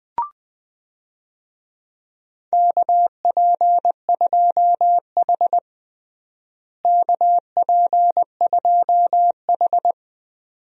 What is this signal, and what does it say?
Morse code, heard as a keyed tone, about 20 wpm, 700 Hz. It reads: KP2H KP2H
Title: DXCC-100 Call Signs - 4 Characters - Repeated 2x - 20wpm